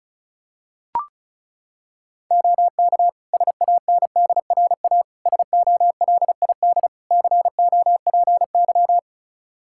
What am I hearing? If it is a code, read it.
Morse code, heard as a keyed tone, about 36 wpm, 700 Hz. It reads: OK SANDRA SOLID COPY
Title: Practice QSO - Exchange 3 - More Info - 35wpm